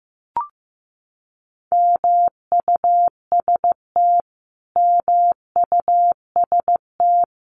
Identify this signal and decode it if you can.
Morse code, heard as a keyed tone, about 15 wpm, 700 Hz. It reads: MUST MUST
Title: Top 500 Words Repeated 15wpm